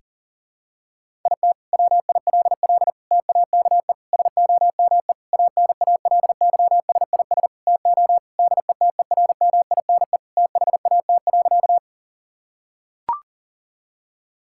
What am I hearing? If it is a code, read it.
Morse code, heard as a keyed tone, about 37 wpm, 700 Hz. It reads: IT WILL TAKE SOME ANALYSIS TO DETERMINE THAT.
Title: Sentences from Top 1000 Words - Review Part 5 - 40wpm